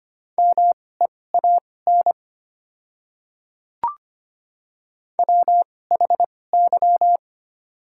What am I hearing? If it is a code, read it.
Morse code, heard as a keyed tone, about 25 wpm, spaced as for 18 wpm, 700 Hz. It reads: MEAN WHY
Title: 2x Character Spacing - Top 300 Words - 25wpm